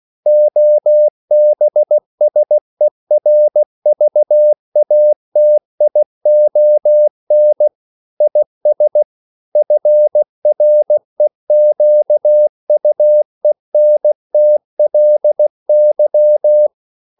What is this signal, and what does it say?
Morse code, heard as a keyed tone, about 16 wpm, 600 Hz. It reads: OBSERVATION IS FREQUENTLY
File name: CW / Practice / EnglishQuotations0116